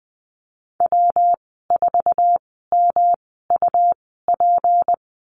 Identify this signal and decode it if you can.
Morse code, heard as a keyed tone, about 20 wpm, spaced as for 14 wpm, 700 Hz. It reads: W4MUP
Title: 186 Lesson 30g - Letter b - Call Signs 20wpm